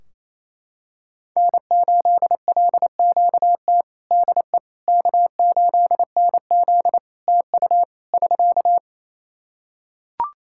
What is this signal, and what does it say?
Morse code, heard as a keyed tone, about 28 wpm, 700 Hz. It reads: N8LQT DE K8NZ TU <SK>
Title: Practice QSO - Exchange 4 - Ending - 28wpm